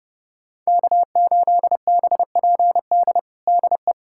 Morse code, heard as a keyed tone, about 30 wpm, 700 Hz. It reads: K8BPD DE
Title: USA Callsign DE Callsign - 5 Characters - 30wpm